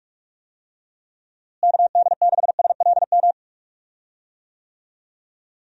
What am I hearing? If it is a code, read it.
Morse code, heard as a keyed tone, about 42 wpm, 700 Hz. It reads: KD6SLM
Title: USA Call Signs - 6 Characters - Repeated 2x - 45wpm